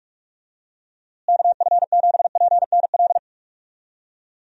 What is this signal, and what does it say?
Morse code, heard as a keyed tone, about 42 wpm, 700 Hz. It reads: KF7PNL
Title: USA Call Signs - 6 Characters - Repeated 2x - 45wpm